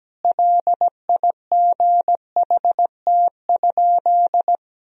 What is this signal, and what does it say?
Morse code, heard as a keyed tone, about 17 wpm, 700 Hz. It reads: LIGHT?
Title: Sentences from Top 1000 Words - Review Part 1 - 17wpm